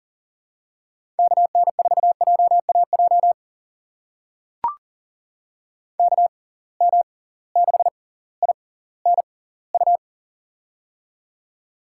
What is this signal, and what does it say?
Morse code, heard as a keyed tone, about 40 wpm, spaced as for 20 wpm, 700 Hz. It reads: KN4JAJ KM6INU
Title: USA Call Signs - 6x Character Spacing - 40wpm